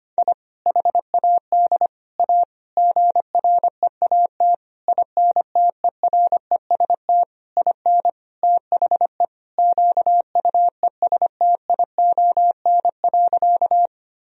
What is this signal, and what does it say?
Morse code, heard as a keyed tone, about 25 wpm, 700 Hz. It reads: I HAD A GREAT INTEREST IN THE QUESTION.
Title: Sentences from Top 1000 Words - Review Part 2 - 25wpm